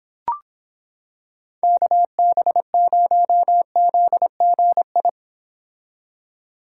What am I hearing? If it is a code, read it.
Morse code, heard as a keyed tone, about 26 wpm, 700 Hz. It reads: KB0ZGI